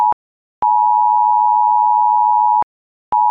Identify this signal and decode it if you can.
Emergency Alert System (SAME) tones (the 853 and 960 Hz attention tone). It is heard from 0.6 to 2.7 s.